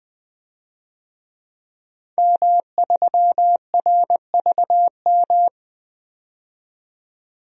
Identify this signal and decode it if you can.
Morse code, heard as a keyed tone, about 20 wpm, 700 Hz. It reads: M3RVM
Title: DXCC-100 Call Signs - 5 Characters - 20wpm